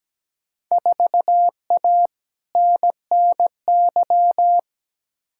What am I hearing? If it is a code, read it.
Morse code, heard as a keyed tone, about 17 wpm, 700 Hz. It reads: 4A NNY
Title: Field Day Exchanges - 17wpm